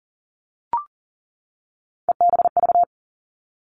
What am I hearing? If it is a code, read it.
Morse code, heard as a keyed tone, about 38 wpm, 700 Hz. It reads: EBV
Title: Three Challenging Letter-Number Combinations 40wpm